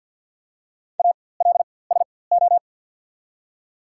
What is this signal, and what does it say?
Morse code, heard as a keyed tone, about 54 wpm, spaced as for 24 wpm, 690 Hz. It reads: ALSO
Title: 4x Character Spacing - Top 300 Words - 50wpm